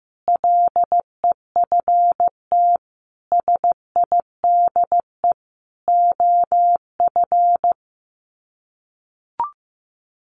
Morse code, heard as a keyed tone, about 15 wpm, 700 Hz. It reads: LEFT SIDE OF
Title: Sets of 5 Words Encore2 15wpm